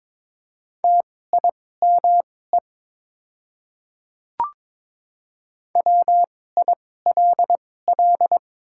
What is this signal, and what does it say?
Morse code, heard as a keyed tone, about 21 wpm, spaced as for 16 wpm, 700 Hz. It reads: TIME WILL
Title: 2x Character Spacing - Top 100 Words - 22wpm